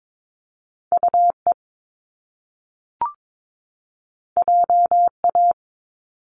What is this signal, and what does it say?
Morse code, heard as a keyed tone, about 22 wpm, 700 Hz. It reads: UE JA
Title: Top Two Letter Combinations in Words 22wpm